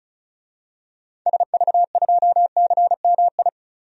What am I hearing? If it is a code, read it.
Morse code, heard as a keyed tone, about 34 wpm, 700 Hz. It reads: SV2CMI
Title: DXCC-100 Call Signs - 6 Characters - 35wpm